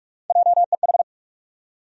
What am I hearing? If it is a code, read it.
Morse code, heard as a keyed tone, about 46 wpm, 700 Hz. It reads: JEH
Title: USA Call Sign Postfixes - 3 Letters 45wpm